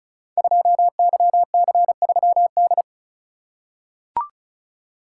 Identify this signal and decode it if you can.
Morse code, heard as a keyed tone, about 36 wpm, 700 Hz. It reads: 2YC3D
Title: Code Groups - 5 Letter:Numbers - 35wpm